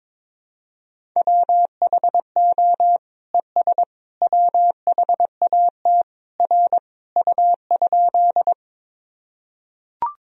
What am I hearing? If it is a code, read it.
Morse code, heard as a keyed tone, about 22 wpm, 700 Hz. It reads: WHO ES WHAT R U?
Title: Sentences from Top 100 Words and Abbreviations 22wpm